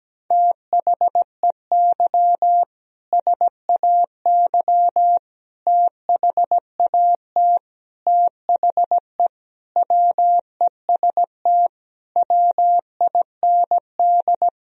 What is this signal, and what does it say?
Morse code, heard as a keyed tone, about 17 wpm, 700 Hz. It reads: THEY SAY THAT THE WEST WIND
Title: Sentences from Top 1000 Words - Review Part 3 - 17wpm